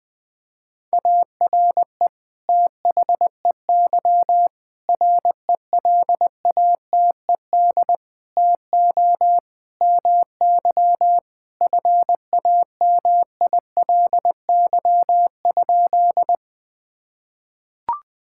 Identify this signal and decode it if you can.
Morse code, heard as a keyed tone, about 20 wpm, 700 Hz. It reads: ARE THEY RELATED TO MY FAMILY?
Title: Sentences from Top 1000 Words - Review Part 5 - 20wpm